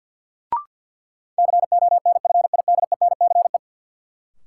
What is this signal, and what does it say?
Morse code, heard as a keyed tone, about 46 wpm, 700 Hz. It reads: CONFIDENCE
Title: Sentences from Top 1500 Words - New Word Drill - 50wpm